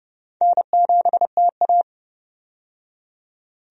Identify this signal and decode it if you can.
Morse code, heard as a keyed tone, about 32 wpm, 700 Hz. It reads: N7TA
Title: USA Call Signs - 4 Characters - 30wpm